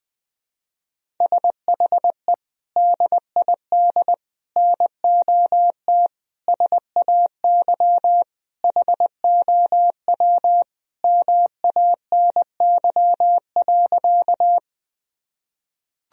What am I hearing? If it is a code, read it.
Morse code, heard as a keyed tone, about 20 wpm, 700 Hz. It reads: SHE DID NOT SAY HOW MANY.
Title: Sentences from Top 1000 Words - Review Part 1 - 20wpm